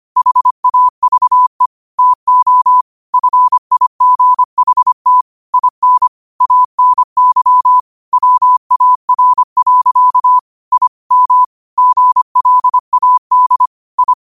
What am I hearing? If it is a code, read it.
Morse code, heard as a keyed tone, about 25 wpm, 1000 Hz. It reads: HAVE TO FIGHT IN ANY WAR. I M GLAD I
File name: qotdfiles / 25wpm